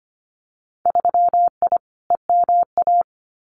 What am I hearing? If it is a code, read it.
Morse code, heard as a keyed tone, about 25 wpm, 700 Hz. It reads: 3I EMA
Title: Winter Field Day Exchanges Encore 25wpm